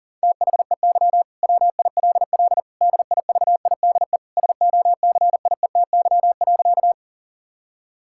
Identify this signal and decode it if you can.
Morse code, heard as a keyed tone, about 37 wpm, 700 Hz. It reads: THEY WILL DIVIDE SOCIETY.
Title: Sentences from Top 1600 Words 40wpm